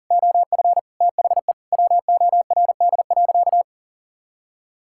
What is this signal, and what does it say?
Morse code, heard as a keyed tone, about 40 wpm, 700 Hz. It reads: OF THE WORD.
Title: Sentences from Top 1000 Words - Review Part 6 - 40wpm